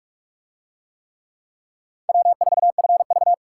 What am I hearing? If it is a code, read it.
Morse code, heard as a keyed tone, about 40 wpm, 700 Hz. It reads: W4FV